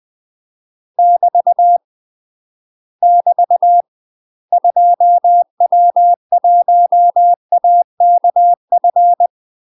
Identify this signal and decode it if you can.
Morse code, heard as a keyed tone, about 20 wpm, 700 Hz. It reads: <BT> <BT> 2W1AKF
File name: UK MASTER QSOs TRAINING / 20WPM / UK MASTER QSOs 0047